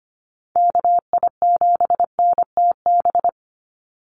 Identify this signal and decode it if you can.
Morse code, heard as a keyed tone, about 25 wpm, 700 Hz. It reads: KI7NTB
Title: USA Call Signs 25wpm